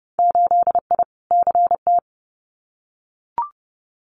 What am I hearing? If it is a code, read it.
Morse code, heard as a keyed tone, about 30 wpm, 700 Hz. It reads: 8I CT
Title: Winter Field Day Exchanges Encore 30wpm